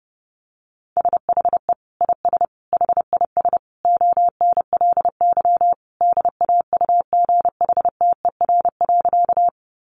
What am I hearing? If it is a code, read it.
Morse code, heard as a keyed tone, about 30 wpm, 700 Hz. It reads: SHE IS HIS ONLY DAUGHTER.
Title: Sentences from Top 1500 Words 30wpm